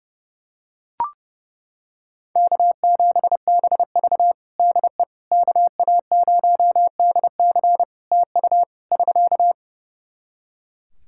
Morse code, heard as a keyed tone, about 30 wpm, 700 Hz. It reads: K7BV DE KA0DC TU <SK>
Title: Practice QSO - Exchange 4 - Ending - 30wpm